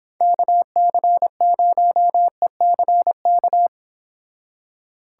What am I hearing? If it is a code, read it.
Morse code, heard as a keyed tone, about 30 wpm, 700 Hz. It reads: KC0ECK